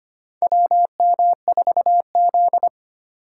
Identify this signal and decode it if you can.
Morse code, heard as a keyed tone, about 25 wpm, 690 Hz. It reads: WM4Z
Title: USA Call Signs - 4 Characters - 4x Character Spacing - 25wpm